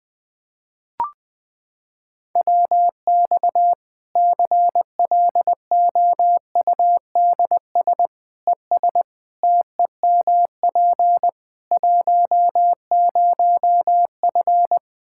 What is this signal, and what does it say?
Morse code, heard as a keyed tone, about 20 wpm, 700 Hz. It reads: WX CLOUDS ES TEMP 10F
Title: Practice QSO - Exchange 2 - Info - 20wpm